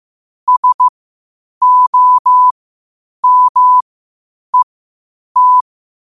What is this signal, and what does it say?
Morse code, heard as a keyed tone, about 15 wpm, spaced as for 8 wpm, 1000 Hz. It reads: SOMET